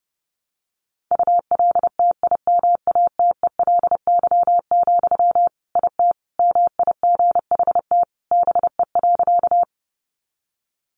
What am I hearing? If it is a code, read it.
Morse code, heard as a keyed tone, about 30 wpm, 700 Hz. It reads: ULTIMATELY, IT MIGHT BE.
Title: Sentences from Top 1200 Words 30wpm